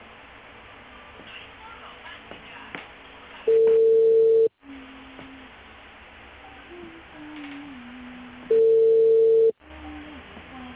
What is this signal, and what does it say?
Ringback tone (China, 450 Hz), with speech.